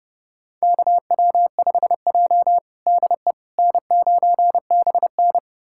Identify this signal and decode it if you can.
Morse code, heard as a keyed tone, about 30 wpm, 700 Hz. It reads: KW5J DE N9BN
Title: USA Callsign DE Callsign - 4 Characters - 30wpm